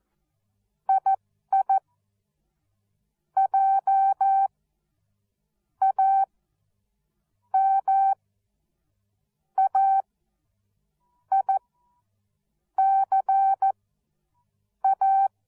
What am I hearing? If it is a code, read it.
Morse code, heard as a keyed tone, about 14 wpm, spaced as for 6 wpm, 780 Hz. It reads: II JAMAICA
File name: Onlinelehrgang / morsen / DJ4UF Morsen13